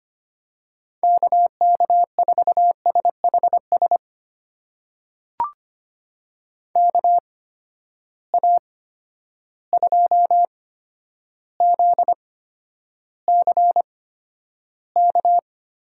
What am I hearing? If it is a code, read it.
Morse code, heard as a keyed tone, about 25 wpm, spaced as for 11 wpm, 700 Hz. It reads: KK4SHS KA2ZCK
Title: USA Call Signs - 6 Characters - 8x Character Spacing - 25wpm